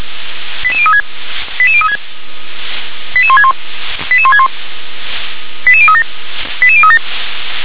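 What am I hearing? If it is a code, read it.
Five-tone selective call (ZVEI), heard as quick runs of five tones, tones of about 70 ms. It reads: 88036 88036 88161 88161 88037 88037
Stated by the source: ZVEI five-tone selective call received by SDR.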